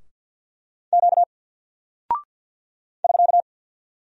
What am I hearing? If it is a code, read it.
Morse code, heard as a keyed tone, about 49 wpm, 700 Hz. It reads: Q <SK>
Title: Single Character (Letter:Number:Punctuation:Prosign) Alt 50wpm